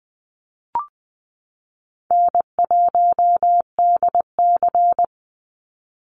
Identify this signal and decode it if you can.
Morse code, heard as a keyed tone, about 20 wpm, 700 Hz. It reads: N1DC
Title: Easy Call Signs - 20wpm